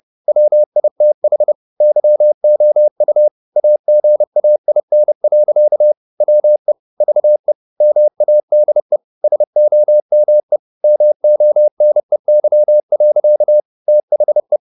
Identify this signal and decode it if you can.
Morse code, heard as a keyed tone, about 30 wpm, 600 Hz. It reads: WITH YOU AGAIN. WE VE MADE SOME MONEY. THE
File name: the old man and the sea / the old man and the sea 1 30wpm